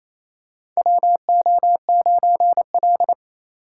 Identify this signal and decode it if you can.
Morse code, heard as a keyed tone, about 28 wpm, 700 Hz. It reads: WO9L
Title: USA Call Signs - 4 Characters - 6x Character Spacing - 28wpm